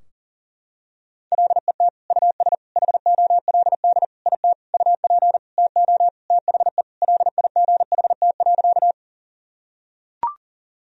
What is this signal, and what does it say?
Morse code, heard as a keyed tone, about 40 wpm, 700 Hz. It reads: LET US HOLD IT UP TO THE LIGHT.
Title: Sentences from Top 1000 Words - Review Part 2 - 40wpm